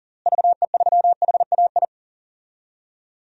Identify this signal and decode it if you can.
Morse code, heard as a keyed tone, about 40 wpm, 700 Hz. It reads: VE3HAI
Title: DXCC-100 Call Signs - 6 Characters - Repeated 2x - 40wpm